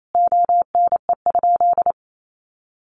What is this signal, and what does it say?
Morse code, heard as a keyed tone, about 28 wpm, 700 Hz. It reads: ONE?
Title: Sentences from Top 900 Words 28wpm